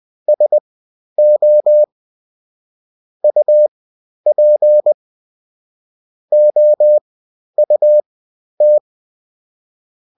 Morse code, heard as a keyed tone, about 20 wpm, spaced as for 11 wpm, 600 Hz. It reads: SO UP OUT
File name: Top-100-Words-10